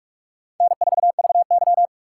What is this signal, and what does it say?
Morse code, heard as a keyed tone, about 46 wpm, 700 Hz. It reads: N4VY
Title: USA Call Signs - 4 Characters - 3x Character Spacing - 45wpm